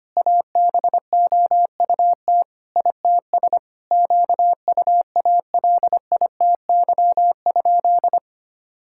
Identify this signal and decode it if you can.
Morse code, heard as a keyed tone, about 25 wpm, 700 Hz. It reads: ABOUT ITS QUALITY?
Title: Sentences from Top 1000 Words - Review Part 6 - 25wpm